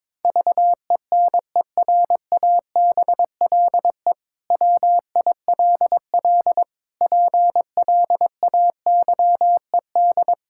Morse code, heard as a keyed tone, about 22 wpm, 700 Hz. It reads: VENERABLE WILL PLAYED
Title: Pangram Challenge v2 - 22wpm